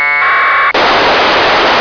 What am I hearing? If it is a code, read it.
Packet radio (AX.25, as in APRS).